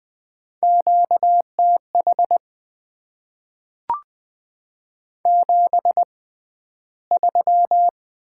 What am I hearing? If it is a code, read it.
Morse code, heard as a keyed tone, about 20 wpm, spaced as for 12 wpm, 700 Hz. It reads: QTH 73
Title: Top 20 QSO Elements - 6x Character Spacing - 20wpm